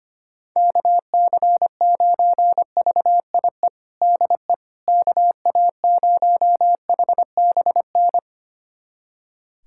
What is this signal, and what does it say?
Morse code, heard as a keyed tone, about 25 wpm, 700 Hz. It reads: KC9VIE DE KA0HBN
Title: USA Callsign DE Callsign - 6 Characters - 25wpm